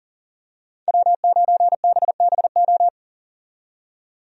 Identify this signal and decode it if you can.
Morse code, heard as a keyed tone, about 37 wpm, 700 Hz. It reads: W9BBO
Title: USA Call Signs - 5 Characters - 40wpm